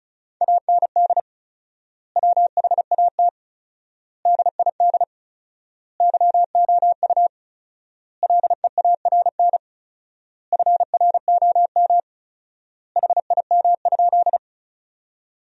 Morse code, heard as a keyed tone, about 34 wpm, 700 Hz. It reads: AND WHAT DID YOU LEARN FROM HIM?
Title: 4x Word-Spacing - Sentences from Top 300 Words - 35wpm